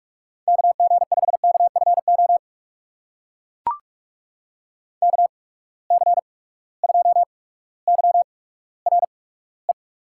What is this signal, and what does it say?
Morse code, heard as a keyed tone, about 44 wpm, spaced as for 19 wpm, 700 Hz. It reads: KG5KFO KC2YRE
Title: USA Call Signs - 6 Characters - 8x Character Spacing - 45wpm